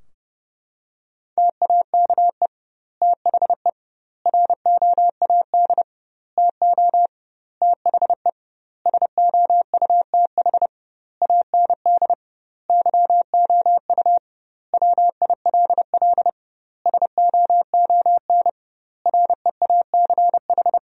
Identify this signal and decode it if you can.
Morse code, heard as a keyed tone, about 30 wpm, 700 Hz. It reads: TAKE THE ROAD TO THE SOUTH AND YOU WILL SOON REACH
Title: 2x Word-Spacing - Sentences from Top 400 Words - 30wpm